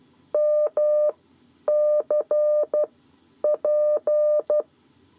Morse code, heard as a keyed tone, about 12 wpm, spaced as for 9 wpm, 600 Hz. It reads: MC P
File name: 7 Lezione cw 2